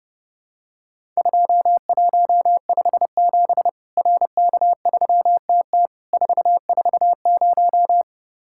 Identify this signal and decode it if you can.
Morse code, heard as a keyed tone, about 30 wpm, 700 Hz. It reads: 2157 RK3TT 440
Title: Worked All Europe DX Contest - QTC Exchange - 30wpm